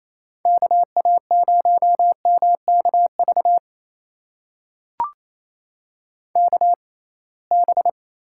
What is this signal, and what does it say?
Morse code, heard as a keyed tone, about 28 wpm, spaced as for 21 wpm, 700 Hz. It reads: KA0MKV KB
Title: USA Call Signs - 6 Characters - 6x Character Spacing - 28wpm